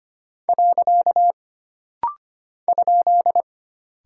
Morse code, heard as a keyed tone, about 25 wpm, 700 Hz. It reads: . ?
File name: Single Punctuation-Prosign - Rapid-Fire - 25wpm